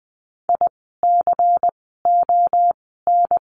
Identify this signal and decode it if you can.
Morse code, heard as a keyed tone, about 20 wpm, spaced as for 15 wpm, 700 Hz. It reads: ICON
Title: 094 Lesson 16f - Letter c - Words 20wpm